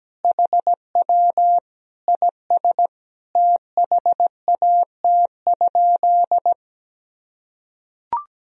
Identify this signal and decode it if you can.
Morse code, heard as a keyed tone, about 17 wpm, 700 Hz. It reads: HW IS THAT?
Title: Sentences from Top 100 Words and Abbreviations 17wpm